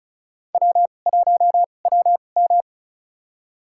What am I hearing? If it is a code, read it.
Morse code, heard as a keyed tone, about 36 wpm, spaced as for 25 wpm, 700 Hz. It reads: W1WM
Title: USA Call Signs - 4 Characters - 2x Character Spacing - 35wpm